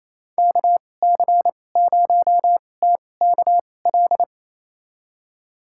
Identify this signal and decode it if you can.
Morse code, heard as a keyed tone, about 28 wpm, spaced as for 20 wpm, 700 Hz. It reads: KC0TKL
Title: USA Call Signs - 6 Characters - 2x Character Spacing - 28wpm